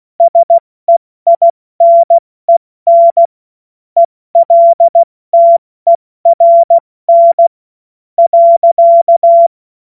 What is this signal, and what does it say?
Morse code, heard as a keyed tone, about 16 wpm, spaced as for 13 wpm, 680 Hz. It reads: SEINEN ELTERN.
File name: Morsekurs / Witze / Witz 24. Januar 2023 70 F80